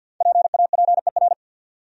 Morse code, heard as a keyed tone, about 68 wpm, spaced as for 59 wpm, 700 Hz. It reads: PAPER
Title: Speed-Racing Top 500 Words 50wpm